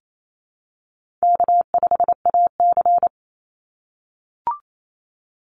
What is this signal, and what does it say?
Morse code, heard as a keyed tone, about 28 wpm, 700 Hz. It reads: K5AC
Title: USA Call Signs 28wpm